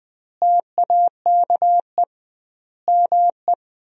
Morse code, heard as a keyed tone, about 20 wpm, 700 Hz. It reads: TAKE ME
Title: 2x Word-Spacing - Sentences from Top 200 Words - 20wpm